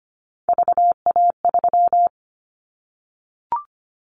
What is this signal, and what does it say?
Morse code, heard as a keyed tone, about 25 wpm, 700 Hz. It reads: VA3
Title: Callsign Prefixes for Top 25 DXCC - Repeated 3x - 25wpm